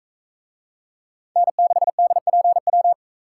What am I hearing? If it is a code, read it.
Morse code, heard as a keyed tone, about 44 wpm, 700 Hz. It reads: N6DPW